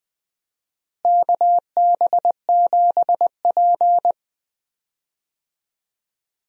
Morse code, heard as a keyed tone, about 20 wpm, 700 Hz. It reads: KB7P
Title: USA Call Signs - 4 Characters - Repeated 3x - 20wpm